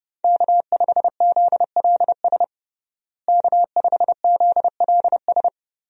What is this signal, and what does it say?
Morse code, heard as a keyed tone, about 32 wpm, 700 Hz. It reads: K5ZLS K5ZLS
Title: USA Call Signs - 5 Characters - Repeated 3x - 30wpm